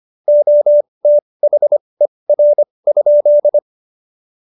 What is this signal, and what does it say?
Morse code, heard as a keyed tone, about 25 wpm, spaced as for 20 wpm, 590 Hz. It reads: OTHER?
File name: Jokes-25-char-20-eff